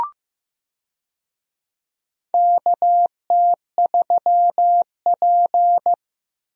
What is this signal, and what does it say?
Morse code, heard as a keyed tone, about 15 wpm, 700 Hz. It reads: KT3P
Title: USA Call Signs - 4 Characters - Repeated 2x - 15wpm